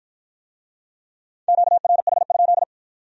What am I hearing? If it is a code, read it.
Morse code, heard as a keyed tone, about 51 wpm, 700 Hz. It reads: QRH?